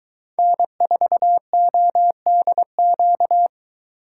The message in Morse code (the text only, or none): N4ODQ